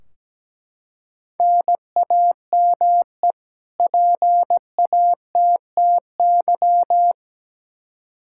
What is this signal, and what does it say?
Morse code, heard as a keyed tone, about 17 wpm, 700 Hz. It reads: NAME PATTY
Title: K1USN SST QSO Practice 17wpm